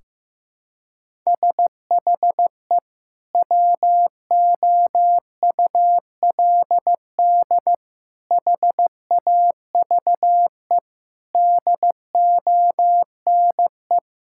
Morse code, heard as a keyed tone, about 15 wpm, 700 Hz. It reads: SHE WOULD HAVE DONE
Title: Sentences from Top 1600 Words 15wpm